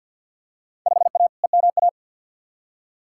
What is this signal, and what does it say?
Morse code, heard as a keyed tone, about 49 wpm, 700 Hz. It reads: 5A EMA
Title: Field Day Exchanges Encore - 50wpm